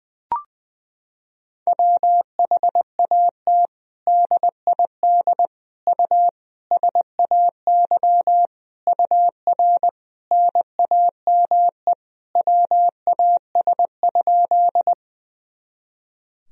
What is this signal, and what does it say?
Morse code, heard as a keyed tone, about 21 wpm, 700 Hz. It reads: WHAT DID U SAY UR NAME WAS?
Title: Sentences from Top 200 Words and Abbreviations 20wpm